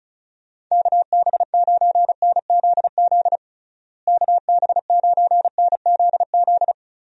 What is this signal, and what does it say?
Morse code, heard as a keyed tone, about 34 wpm, 700 Hz. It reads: KB9NZZ KB9NZZ
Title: USA Call Signs - 6 Characters - Repeated 2x - 35wpm